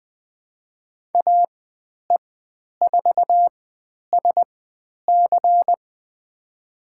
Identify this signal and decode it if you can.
Morse code, heard as a keyed tone, about 20 wpm, spaced as for 10 wpm, 700 Hz. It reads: AE4SC
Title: 102 Lesson 17g - Letter u - Call Signs 20wpm